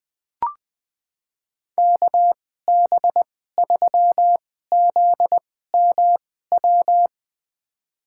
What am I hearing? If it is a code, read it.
Morse code, heard as a keyed tone, about 21 wpm, spaced as for 15 wpm, 700 Hz. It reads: KB3ZMW
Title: USA Call Signs - 6 Characters - 2x Character Spacing - 20wpm